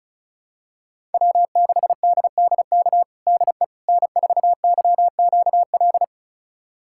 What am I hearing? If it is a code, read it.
Morse code, heard as a keyed tone, about 34 wpm, 700 Hz. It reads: W6DDK DE N4YQL
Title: USA Callsign DE Callsign - 5 Characters - 35wpm